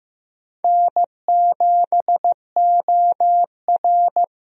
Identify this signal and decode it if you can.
Morse code, heard as a keyed tone, about 15 wpm, 700 Hz. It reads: N7OR
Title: USA Call Signs - 4 Characters - Repeated 3x - 15wpm